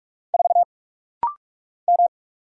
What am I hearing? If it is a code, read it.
Morse code, heard as a keyed tone, about 44 wpm, 690 Hz. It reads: 4 M
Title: Single Letter-Number - Rapid-Fire 45wpm